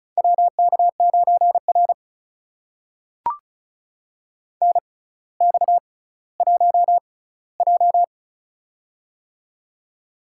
Morse code, heard as a keyed tone, about 36 wpm, spaced as for 18 wpm, 700 Hz. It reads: WK9R NX1J
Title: USA Call Signs - 4 Characters - 6x Character Spacing - 35wpm